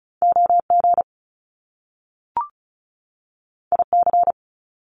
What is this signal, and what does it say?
Morse code, heard as a keyed tone, about 36 wpm, 700 Hz. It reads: OG IC
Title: Top Two Letter Combinations in Words 35wpm